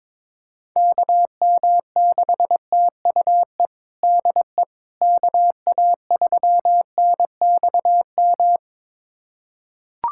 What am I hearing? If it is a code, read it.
Morse code, heard as a keyed tone, about 22 wpm, 700 Hz. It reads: KM6TUE DE KA3NXM
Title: USA Callsign DE Callsign 22wpm